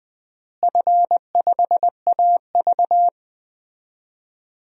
Morse code, heard as a keyed tone, about 21 wpm, 700 Hz. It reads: F5AV
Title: DXCC-100 Call Signs - 4 Characters - Repeated 2x - 20wpm